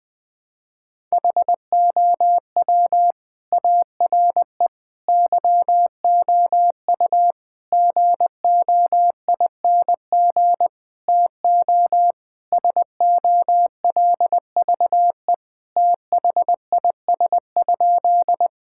Morse code, heard as a keyed tone, about 20 wpm, 700 Hz. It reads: HOW ARE YOU GOING TO SOLVE THIS?
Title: Sentences with 5-Word Phrases - 20wpm